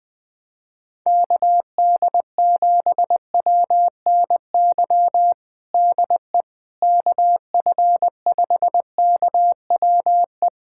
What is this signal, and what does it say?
Morse code, heard as a keyed tone, about 21 wpm, 700 Hz. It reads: KD7WNY DE KF5KWE
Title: USA Callsign DE Callsign - 6 Characters - 20wpm